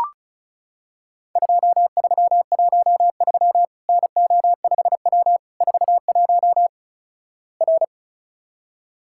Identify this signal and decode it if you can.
Morse code, heard as a keyed tone, about 36 wpm, 700 Hz. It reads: 2313 NO5W 41 R
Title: Worked All Europe DX Contest - QTC Exchange - 35wpm